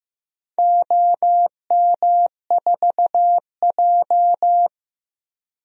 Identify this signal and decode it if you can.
Morse code, heard as a keyed tone, about 15 wpm, 700 Hz. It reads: OM4J
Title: DXCC-100 Call Signs - 4 Characters - Repeated 3x - 15wpm